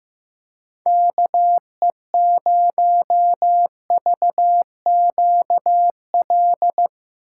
Morse code, heard as a keyed tone, about 15 wpm, 700 Hz. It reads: KE0VQL
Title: USA Call Signs - 6 Characters - 6x Character Spacing - 15wpm